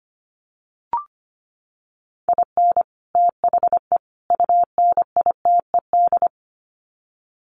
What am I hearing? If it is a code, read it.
Morse code, heard as a keyed tone, about 25 wpm, 700 Hz. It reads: IN THE UNITED
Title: Sets of 3 Words 25wpm